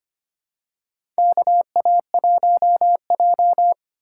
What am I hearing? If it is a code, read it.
Morse code, heard as a keyed tone, about 25 wpm, 700 Hz. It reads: KA1J
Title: USA Call Signs - 3x Character Spacing - 25wpm